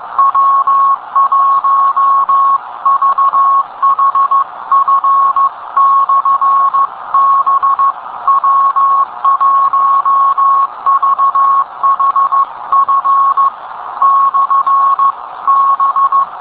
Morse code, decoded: W1VHF/B W1VHF/B